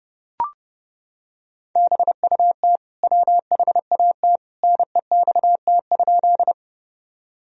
Morse code, heard as a keyed tone, about 32 wpm, 700 Hz. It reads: BUT WHAT NEXT?
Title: Sentences from Top 1000 Words - Review Part 1 - 30wpm